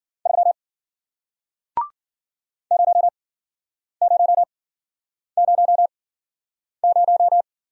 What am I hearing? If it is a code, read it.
Morse code, heard as a keyed tone, about 52 wpm, 700 Hz. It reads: 3 0 0 0 0